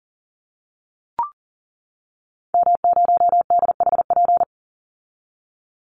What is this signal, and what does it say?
Morse code, heard as a keyed tone, about 40 wpm, 700 Hz. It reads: M0DHP
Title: Call Signs - 40wpm